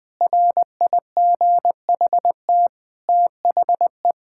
Morse code, heard as a keyed tone, about 21 wpm, 700 Hz. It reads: RIGHT THE
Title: Sentences from Top 1000 Words - Review Part 6 - 20wpm